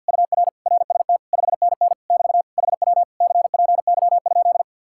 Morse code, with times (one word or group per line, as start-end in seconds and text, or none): 0.1-0.5 UR
0.7-1.2 RST
1.3-1.9 5NN
2.1-2.4 <BT>
2.6-3.0 HW
3.2-4.6 CPY?